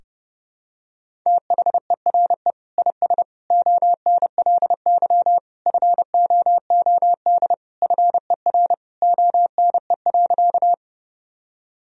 Morse code, heard as a keyed tone, about 30 wpm, 700 Hz. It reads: THERE IS ONLY FOOD FER ONE.
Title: Sentences from Top 300 Words and Abbreviations 30wpm